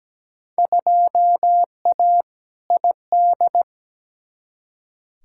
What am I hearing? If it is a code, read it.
Morse code, heard as a keyed tone, about 17 wpm, 700 Hz. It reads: 2A ID
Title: Field Day Exchanges Encore - 17wpm